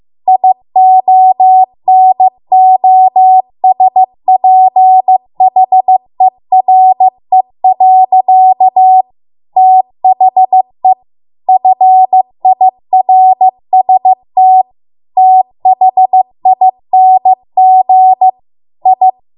Morse code, heard as a keyed tone, about 15 wpm, 750 Hz. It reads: IONOSPHERE. THE FIRST THING I